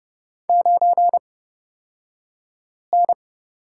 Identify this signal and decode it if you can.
Morse code, heard as a keyed tone, about 31 wpm, spaced as for 5 wpm, 700 Hz. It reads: 9N
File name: Lesson 37d - Letter x - Two Characters 30wpm